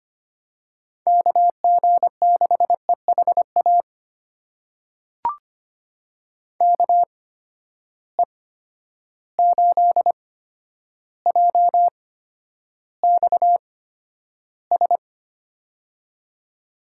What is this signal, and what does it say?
Morse code, heard as a keyed tone, about 25 wpm, spaced as for 11 wpm, 700 Hz. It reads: KG6EHA KE8JXS
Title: USA Call Signs - 6 Characters - 8x Character Spacing - 25wpm